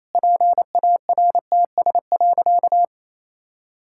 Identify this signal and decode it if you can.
Morse code, heard as a keyed tone, about 27 wpm, 700 Hz. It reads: PARTS.
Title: Sentences from Top 1000 Words - Review Part 6 - 28wpm